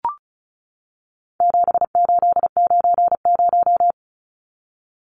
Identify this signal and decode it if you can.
Morse code, heard as a keyed tone, about 34 wpm, 700 Hz. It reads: 7890
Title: Numbers - 4-Digit 35wpm